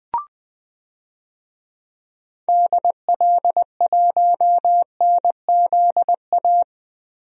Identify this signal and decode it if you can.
Morse code, heard as a keyed tone, about 20 wpm, 700 Hz. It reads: DL1NZA
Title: DXCC-100 Call Signs - 6 Characters - Repeated 3x - 20wpm